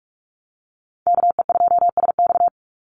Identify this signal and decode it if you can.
Morse code, heard as a keyed tone, about 44 wpm, 700 Hz. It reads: KE2SX
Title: Call Signs - 45wpm